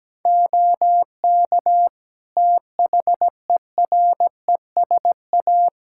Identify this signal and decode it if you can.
Morse code, heard as a keyed tone, about 17 wpm, 700 Hz. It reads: OK THERESA
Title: Practice QSO - Exchange 2 - Info - 17wpm